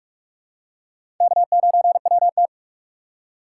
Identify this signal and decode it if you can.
Morse code, heard as a keyed tone, about 43 wpm, 700 Hz. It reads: K9WT
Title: USA Call Signs - 4 Characters - 4x Character Spacing - 45wpm